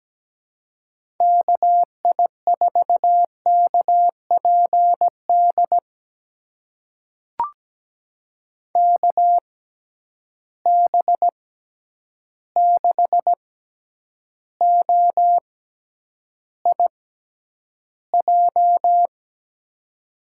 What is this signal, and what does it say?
Morse code, heard as a keyed tone, about 17 wpm, spaced as for 9 wpm, 700 Hz. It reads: KI4KPD KB6OIJ
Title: USA Call Signs - 6x Character Spacing - 17wpm